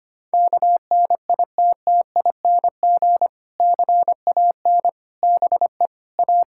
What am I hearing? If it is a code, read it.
Morse code, heard as a keyed tone, about 25 wpm, 700 Hz. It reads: KNITTING CAN BE A
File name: practice / 25wpm / Difficult Sentences with Hints - 25wpm